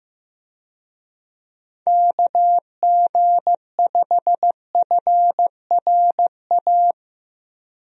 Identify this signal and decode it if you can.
Morse code, heard as a keyed tone, about 15 wpm, 700 Hz. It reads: KG5FRA
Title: USA Call Signs - 6 Characters - 15wpm